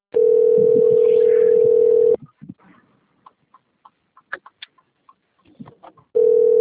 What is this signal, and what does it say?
Ringback tone (North America), 440 + 480 Hz.
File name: rg-1600-unknown- - - .9262